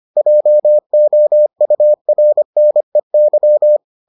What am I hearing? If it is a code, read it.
Morse code, heard as a keyed tone, about 25 wpm, 600 Hz. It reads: JOURNEY